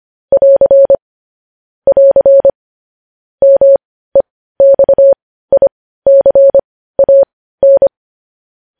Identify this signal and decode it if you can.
Morse code, heard as a keyed tone, about 25 wpm, spaced as for 15 wpm, 550 Hz. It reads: <AR> <AR> MEXICAN
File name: podcast / News headlines-1x-25x15